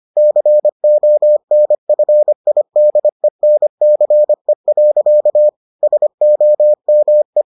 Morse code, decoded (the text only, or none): CONFIDENCE. SOME